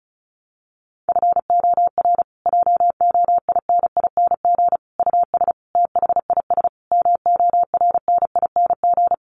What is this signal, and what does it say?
Morse code, heard as a keyed tone, about 34 wpm, 700 Hz. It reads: FOR JOINING US THIS MORNING
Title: Sets of 5 Words Encore 35wpm